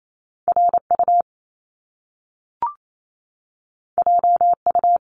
Morse code, heard as a keyed tone, about 28 wpm, 700 Hz. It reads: RU JU
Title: Top Two Letter Combinations in Words 28wpm